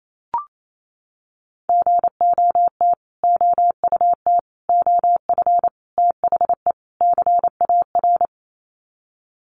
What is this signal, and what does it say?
Morse code, heard as a keyed tone, about 28 wpm, 700 Hz. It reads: GOT OUT OF THE CAR
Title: Sets of 5 Words Encore 28wpm